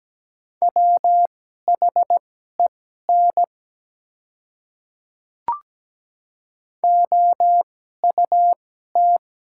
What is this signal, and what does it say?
Morse code, heard as a keyed tone, about 17 wpm, spaced as for 12 wpm, 700 Hz. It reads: WHEN OUT
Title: 2x Character Spacing - Top 100 Words - 17wpm